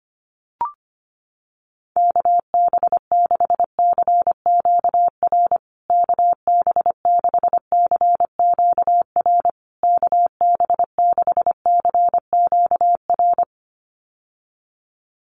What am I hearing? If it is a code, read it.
Morse code, heard as a keyed tone, about 25 wpm, 700 Hz. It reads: KB6CQR KB6CQR KB6CQR
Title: USA Call Signs Repeated 3x 25wpm